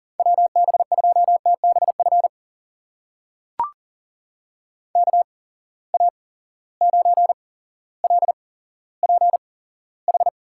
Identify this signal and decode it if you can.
Morse code, heard as a keyed tone, about 40 wpm, spaced as for 17 wpm, 700 Hz. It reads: WB2TBF KA9LPH
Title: USA Call Signs - 6 Characters - 8x Character Spacing - 40wpm